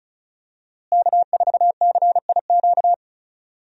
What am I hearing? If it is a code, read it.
Morse code, heard as a keyed tone, about 34 wpm, 700 Hz. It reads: K4CIQ
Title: USA Call Signs - 5 Characters - Repeated 3x - 35wpm